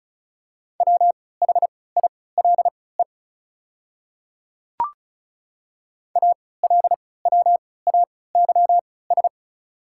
Morse code, heard as a keyed tone, about 34 wpm, spaced as for 20 wpm, 700 Hz. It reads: WHILE ALWAYS
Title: 3x Character Spacing - Top 500 Words - 35wpm